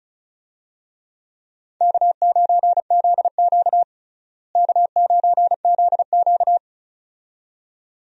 Morse code, heard as a keyed tone, about 36 wpm, 700 Hz. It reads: K9ZQ K9ZQ
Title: USA Call Signs - 4 Characters - Repeated 2x - 35wpm